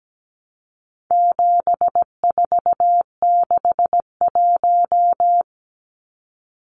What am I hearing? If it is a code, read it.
Morse code, heard as a keyed tone, about 17 wpm, 700 Hz. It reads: 7461
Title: Numbers - 4-Digit 17wpm